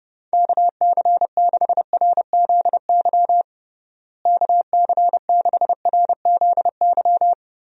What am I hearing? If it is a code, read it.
Morse code, heard as a keyed tone, about 30 wpm, 700 Hz. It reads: KC6RZY KC6RZY
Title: USA Call Signs - 6 Characters - Repeated 3x - 30wpm